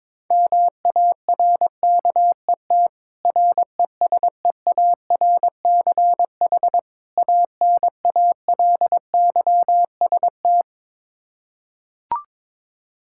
Morse code, heard as a keyed tone, about 22 wpm, 700 Hz. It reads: MARKET RESEARCH ANALYST
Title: Practice QSO - Exchange 3 - More Info - 22wpm